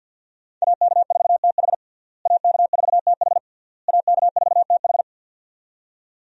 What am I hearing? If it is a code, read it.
Morse code, heard as a keyed tone, about 52 wpm, 700 Hz. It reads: AK4TH AK4TH AK4TH
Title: USA Call Signs - 5 Characters - Repeated 3x - 50wpm